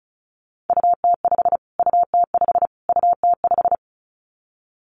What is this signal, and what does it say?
Morse code, heard as a keyed tone, about 34 wpm, 700 Hz. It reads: UT5 UT5 UT5
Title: Callsign Prefixes for Top 25 DXCC - Repeated 3x - 35wpm